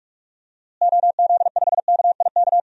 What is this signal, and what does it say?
Morse code, heard as a keyed tone, about 48 wpm, 700 Hz. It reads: OZ5KIK
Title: DXCC-100 Call Signs - 6 Characters - Repeated 2x - 45wpm